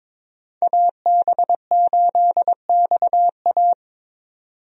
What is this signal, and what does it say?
Morse code, heard as a keyed tone, about 22 wpm, 700 Hz. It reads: AB8XA
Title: USA Call Signs - 5 Characters - 22wpm